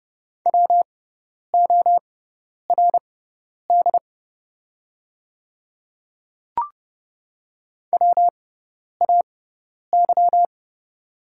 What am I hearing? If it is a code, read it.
Morse code, heard as a keyed tone, about 30 wpm, spaced as for 10 wpm, 700 Hz. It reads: WORD WAY
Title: 6x Character Spacing - Top 200 Words - 30wpm